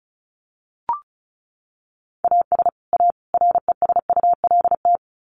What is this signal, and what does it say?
Morse code, heard as a keyed tone, about 34 wpm, 700 Hz. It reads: AS A RESULT
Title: Sets of 3 Words 35wpm